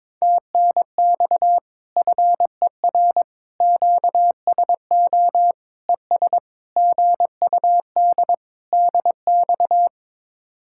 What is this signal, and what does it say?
Morse code, heard as a keyed tone, about 23 wpm, 700 Hz. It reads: TNX FER QSO ES GUD DX
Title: Speed-Racing Common QSO Phrases - 22wpm